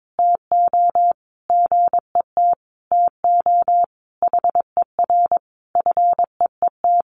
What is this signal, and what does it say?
Morse code, heard as a keyed tone, about 22 wpm, 700 Hz. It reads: TO GET TO HER FEET
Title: Sentences from Top 1300 Words 22wpm